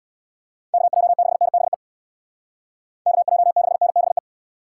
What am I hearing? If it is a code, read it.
Morse code, heard as a keyed tone, about 65 wpm, 700 Hz. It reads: KJ6NBE KJ6NBE